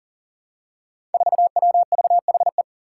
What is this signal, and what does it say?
Morse code, heard as a keyed tone, about 39 wpm, 700 Hz. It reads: 4WVHE
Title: Five Challenging Letter-Number Combinations 40wpm